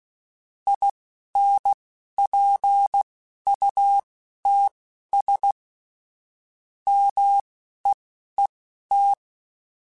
Morse code, heard as a keyed tone, about 16 wpm, spaced as for 11 wpm, 800 Hz. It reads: INPUTS MEET
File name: Technical Subjects / Technical Subjects-mp3s / OR Gate @10